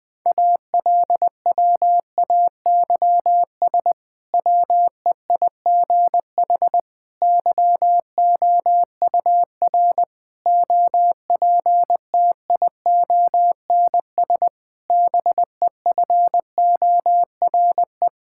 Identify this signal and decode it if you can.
Morse code, heard as a keyed tone, about 20 wpm, 700 Hz. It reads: ALWAYS WEIGH YOUR OPTIONS BEFORE
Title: Difficult Sentences with Hints - 20wpm